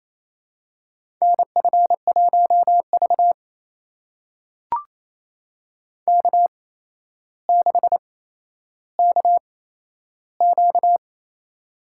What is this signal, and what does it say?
Morse code, heard as a keyed tone, about 28 wpm, spaced as for 12 wpm, 700 Hz. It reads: NF1V K6KQ
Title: USA Call Signs - 4 Characters - 8x Character Spacing - 28wpm